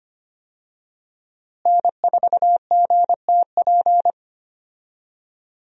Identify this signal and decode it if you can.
Morse code, heard as a keyed tone, about 25 wpm, 700 Hz. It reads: N4GTP
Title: USA Call Signs - 5 Characters - Repeated 2x - 25wpm